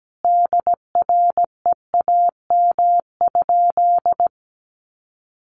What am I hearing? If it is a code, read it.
Morse code, heard as a keyed tone, about 17 wpm, 700 Hz. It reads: DREAM?
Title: Sentences from Top 1500 Words 17wpm